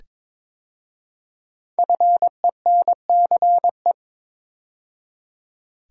Morse code, heard as a keyed tone, about 22 wpm, 700 Hz. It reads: FENCE
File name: 5 Letter Words - First 3 Letters Given - 22wpm